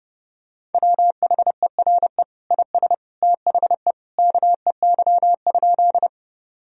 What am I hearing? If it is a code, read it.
Morse code, heard as a keyed tone, about 32 wpm, 700 Hz. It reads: WHERE IS THE KEY?
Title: Sentences from Top 1000 Words - Review Part 4 - 30wpm